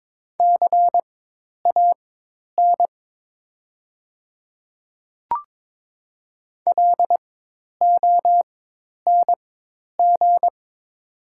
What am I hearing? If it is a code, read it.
Morse code, heard as a keyed tone, about 22 wpm, spaced as for 10 wpm, 700 Hz. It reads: CAN LONG
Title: 4x Character Spacing - Top 100 Words - 22wpm